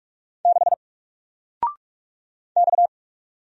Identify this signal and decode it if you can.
Morse code, heard as a keyed tone, about 44 wpm, 700 Hz. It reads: 6 X
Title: Single Character (Letter:Number:Punctuation:Prosign) Alt 45wpm